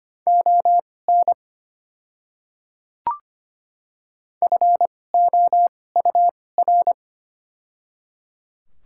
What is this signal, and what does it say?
Morse code, heard as a keyed tone, about 25 wpm, spaced as for 18 wpm, 700 Hz. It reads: ON FOUR
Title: 2x Character Spacing - Top 300 Words - 25wpm